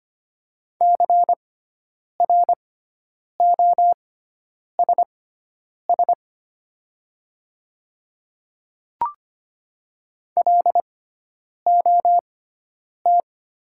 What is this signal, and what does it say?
Morse code, heard as a keyed tone, about 25 wpm, spaced as for 9 wpm, 690 Hz. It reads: CROSS LOT
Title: 6x Character Spacing - Top 500 Words - 25wpm